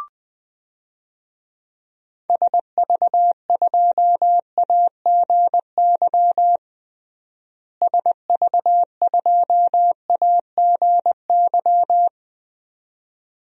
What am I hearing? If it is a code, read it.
Morse code, heard as a keyed tone, about 19 wpm, 700 Hz. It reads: SV2AGY SV2AGY
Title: DXCC-100 Call Signs - 6 Characters - Repeated 2x - 20wpm